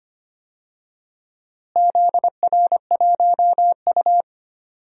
Morse code, heard as a keyed tone, about 25 wpm, 700 Hz. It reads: ZR1U
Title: DXCC-100 Call Signs - 4 Characters - Repeated 2x - 25wpm